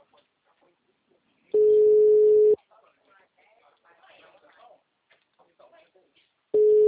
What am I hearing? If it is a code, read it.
Ringback tone (Europe, 425 Hz), with speech.
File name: out- -2002- - - .89702